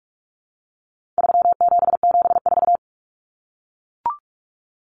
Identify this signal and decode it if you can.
Morse code, heard as a keyed tone, about 44 wpm, 700 Hz. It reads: 3774
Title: Common Cut Numbers - 4-Digit 45wpm